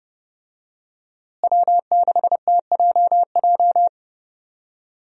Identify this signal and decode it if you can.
Morse code, heard as a keyed tone, about 30 wpm, 700 Hz. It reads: W6TJJ
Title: USA Call Signs - 5 Characters - Repeated 3x - 30wpm